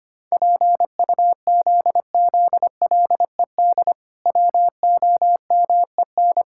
Morse code, heard as a keyed tone, about 25 wpm, 700 Hz. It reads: PUZZLED WOMEN
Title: Pangram Challenge v2 - 25wpm